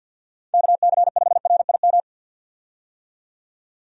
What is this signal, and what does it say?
Morse code, heard as a keyed tone, about 50 wpm, 700 Hz. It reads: KC5RIM
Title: USA Call Signs - 6 Characters - 50wpm